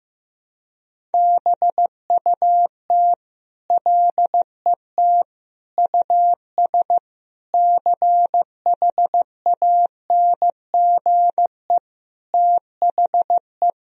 Morse code, heard as a keyed tone, about 15 wpm, 700 Hz. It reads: BUT LET US CHANGE THE
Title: Sentences from Top 1000 Words - Review Part 1 - 15wpm